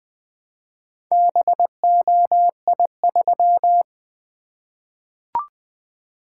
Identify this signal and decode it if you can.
Morse code, heard as a keyed tone, about 21 wpm, 700 Hz. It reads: BOI3
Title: Four Letter-Number Combinations - 20wpm